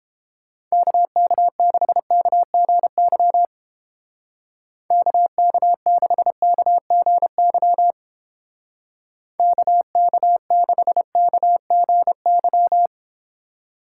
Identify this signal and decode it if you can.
Morse code, heard as a keyed tone, about 29 wpm, 700 Hz. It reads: KK6KGY KK6KGY KK6KGY